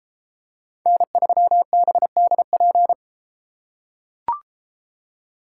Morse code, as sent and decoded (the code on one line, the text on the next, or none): -. ...-- -... -.. .--.
N3BDP